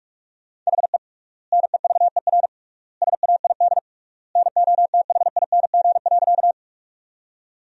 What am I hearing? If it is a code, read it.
Morse code, heard as a keyed tone, about 46 wpm, 700 Hz. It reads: HE NEVER SAID NOTHING.
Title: 3x Word-Spacing - Sentences from Top 500 Words - 45wpm